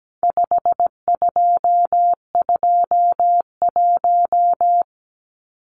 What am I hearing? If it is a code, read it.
Morse code, heard as a keyed tone, about 17 wpm, 700 Hz. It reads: 5221
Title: Numbers - 4-Digit 17wpm